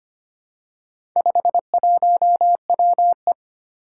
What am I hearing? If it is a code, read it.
Morse code, heard as a keyed tone, about 25 wpm, 700 Hz. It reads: 51WE
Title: Four Challenging Letter-Number Combinations 25wpm